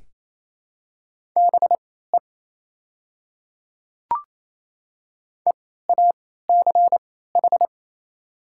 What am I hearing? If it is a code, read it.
Morse code, heard as a keyed tone, about 28 wpm, spaced as for 16 wpm, 700 Hz. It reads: BE EACH
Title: 3x Character Spacing - Top 100 Words - 28wpm